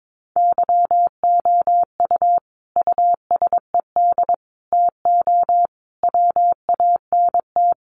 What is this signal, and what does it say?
Morse code, heard as a keyed tone, about 22 wpm, 700 Hz. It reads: YOU USED TO WANT
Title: Sentences from Top 600 Words 22wpm